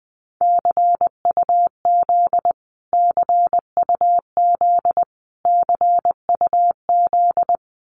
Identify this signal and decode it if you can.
Morse code, heard as a keyed tone, about 20 wpm, 700 Hz. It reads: CUZ CUZ CUZ
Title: Abbreviations Repeated 3x - 20wpm